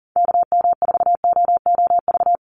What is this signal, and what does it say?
Morse code, heard as a keyed tone, about 40 wpm, 700 Hz. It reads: KM4OOV
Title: USA Call Signs 40wpm